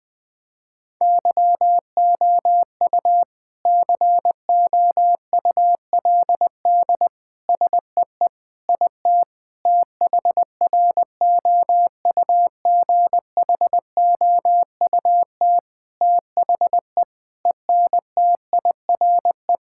Morse code, heard as a keyed tone, about 20 wpm, 700 Hz. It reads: YOU COULD SEE IT THROUGHOUT THE ENTIRE
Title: Sentences from Top 1000 Words - Review Part 5 - 20wpm